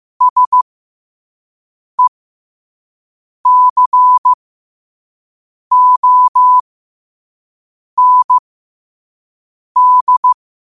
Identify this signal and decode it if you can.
Morse code, heard as a keyed tone, about 15 wpm, spaced as for 5 wpm, 1000 Hz. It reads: SECOND